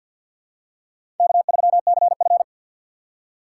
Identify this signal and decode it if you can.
Morse code, heard as a keyed tone, about 51 wpm, 700 Hz. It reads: K3CF
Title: USA Call Signs - 4 Characters - 2x Character Spacing - 50wpm